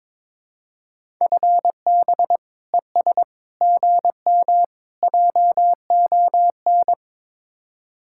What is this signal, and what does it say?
Morse code, heard as a keyed tone, about 23 wpm, 700 Hz. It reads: FB ES GM JON
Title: Speed-Racing Common QSO Phrases - 22wpm